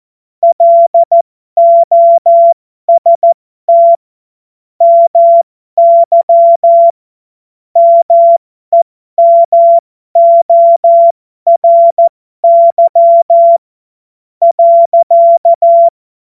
Morse code, heard as a keyed tone, about 14 wpm, spaced as for 12 wpm, 680 Hz. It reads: LOST MY MEMORY .
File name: Morsekurs / Witze